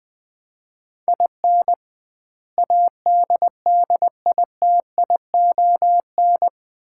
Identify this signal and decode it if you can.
Morse code, heard as a keyed tone, about 21 wpm, 700 Hz. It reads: IN ADDITION
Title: 2x Word-Spacing - Sets of 4 Words - 20wpm